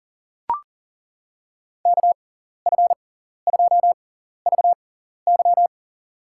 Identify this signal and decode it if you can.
Morse code, heard as a keyed tone, about 37 wpm, spaced as for 14 wpm, 700 Hz. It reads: KF2VY
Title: USA Call Signs - 5 Characters - 6x Character Spacing - 40wpm